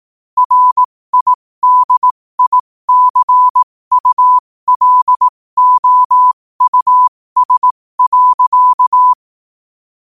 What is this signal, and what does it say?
Morse code, heard as a keyed tone, about 18 wpm, spaced as for 15 wpm, 1000 Hz. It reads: RIDICULOUS.
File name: qotdfiles / 15wpm